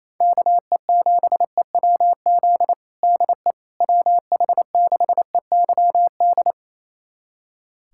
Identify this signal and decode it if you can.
Morse code, heard as a keyed tone, about 28 wpm, 700 Hz. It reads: KE7EWZ DE WH6EYD
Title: USA Callsign DE Callsign 28wpm